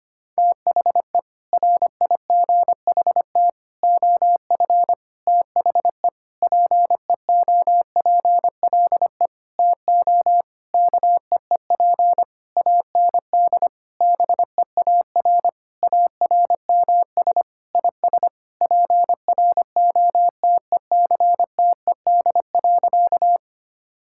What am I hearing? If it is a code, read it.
Morse code, heard as a keyed tone, about 25 wpm, 700 Hz. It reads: THE RIGHT OF THE PEOPLE TO KEEP AND BEAR ARMS IS PROTECTED.
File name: practice / 25wpm / Sentences with 9-Word Phrases - 25wpm